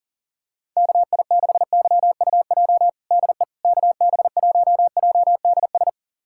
Morse code, heard as a keyed tone, about 40 wpm, 700 Hz. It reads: KI6YUJ DE KB1JDS
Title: USA Callsign DE Callsign 40wpm